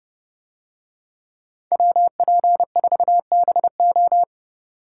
Morse code, heard as a keyed tone, about 30 wpm, 700 Hz. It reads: WP4BO